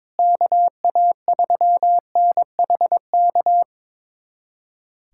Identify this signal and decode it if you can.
Morse code, heard as a keyed tone, about 22 wpm, 700 Hz. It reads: KA3NHK
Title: USA Callsign DE Callsign 22wpm